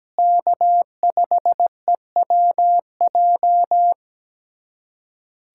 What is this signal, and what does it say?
Morse code, heard as a keyed tone, about 17 wpm, 700 Hz. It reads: K5EWJ
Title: USA Callsign DE Callsign - 5 Characters - 17wpm